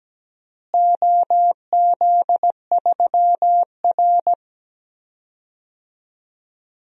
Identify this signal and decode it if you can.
Morse code, heard as a keyed tone, about 17 wpm, 700 Hz. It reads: OZ3R
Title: DXCC-100 Call Signs - 4 Characters - Repeated 3x - 17wpm